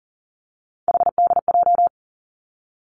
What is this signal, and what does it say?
Morse code, heard as a keyed tone, about 43 wpm, 700 Hz. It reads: HDJ
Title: Three Challenging Letter-Number Combinations 40wpm